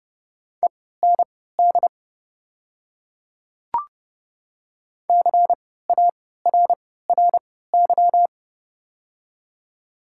Morse code, heard as a keyed tone, about 30 wpm, spaced as for 17 wpm, 690 Hz. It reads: END CARRY
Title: 3x Character Spacing - Top 300 Words - 30wpm